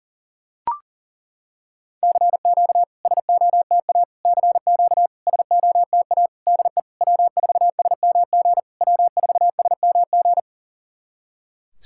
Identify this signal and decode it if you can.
Morse code, heard as a keyed tone, about 40 wpm, 700 Hz. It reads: CQ SOTA CQ SOTA DE W4SMG W4SMG
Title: SOTA QSO Practice 40wpm